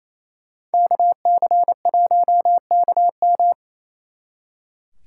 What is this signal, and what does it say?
Morse code, heard as a keyed tone, about 32 wpm, 700 Hz. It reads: KC1KM
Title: Speed Racing - USA 2x2 Call Signs 28wpm